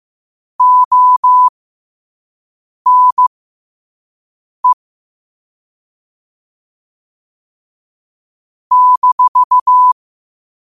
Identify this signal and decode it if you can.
Morse code, heard as a keyed tone, about 15 wpm, spaced as for 5 wpm, 1000 Hz. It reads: ONE -
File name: qotdfiles / 5wpm